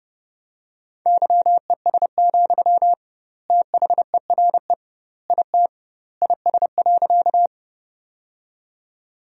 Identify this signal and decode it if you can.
Morse code, heard as a keyed tone, about 30 wpm, 700 Hz. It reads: YES, THERE IT IS.
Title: 2x Word-Spacing - Sentences from Top 500 Words - 30wpm